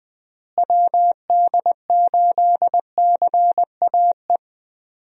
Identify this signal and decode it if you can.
Morse code, heard as a keyed tone, about 20 wpm, 700 Hz. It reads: WD8CAE
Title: USA Call Signs - 6 Characters - Repeated 3x - 20wpm